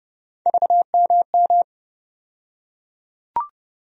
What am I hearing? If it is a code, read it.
Morse code, heard as a keyed tone, about 31 wpm, 700 Hz. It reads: VMM
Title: USA Call Sign Postfixes - 3 Letters 30wpm